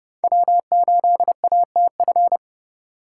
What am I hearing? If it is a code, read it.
Morse code, heard as a keyed tone, about 30 wpm, 700 Hz. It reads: W8ATF